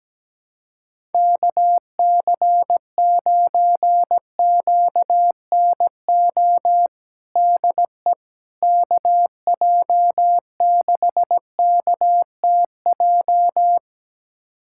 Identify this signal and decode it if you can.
Morse code, heard as a keyed tone, about 17 wpm, 700 Hz. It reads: KC9QNO DE KJ6KTJ
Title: USA Callsign DE Callsign 17wpm